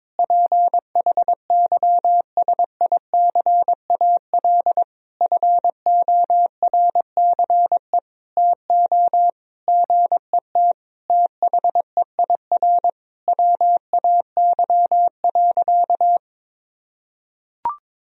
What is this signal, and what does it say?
Morse code, heard as a keyed tone, about 22 wpm, 700 Hz. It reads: PHYSICAL FORCE TO GET THEIR WAY.
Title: Sentences from Top 1000 Words - Review Part 4 - 22wpm